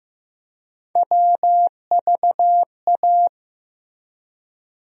Morse code, heard as a keyed tone, about 15 wpm, 700 Hz. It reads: WVA
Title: USA Call Sign Postfixes - 3 Letters 15wpm